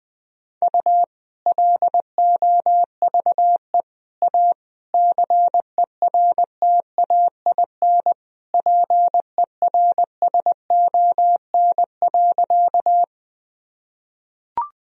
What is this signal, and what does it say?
Morse code, heard as a keyed tone, about 20 wpm, 700 Hz. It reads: U LOVE A CERTAIN PERSON.
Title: Sentences from Top 500 Words and Abbreviations 20wpm